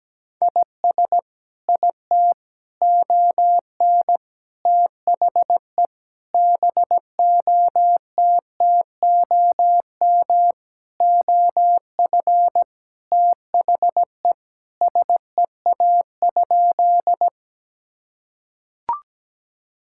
Morse code, heard as a keyed tone, about 17 wpm, 700 Hz. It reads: IS IT ON THE BOTTOM OF THE SEA?
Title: Sentences from Top 1000 Words - Review Part 6 - 17wpm